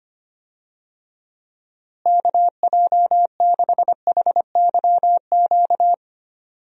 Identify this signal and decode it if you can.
Morse code, heard as a keyed tone, about 25 wpm, 700 Hz. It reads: KJ6HYQ
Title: USA Call Signs - 6 Characters - Repeated 2x - 25wpm